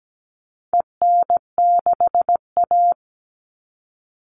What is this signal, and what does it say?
Morse code, heard as a keyed tone, about 17 wpm, 700 Hz. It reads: EN6A
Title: All Cut Numbers - 4-Digit 17wpm